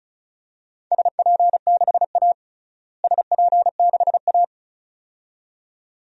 Morse code, heard as a keyed tone, about 34 wpm, 700 Hz. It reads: SP6A SP6A
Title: DXCC-100 Call Signs - 4 Characters - Repeated 2x - 35wpm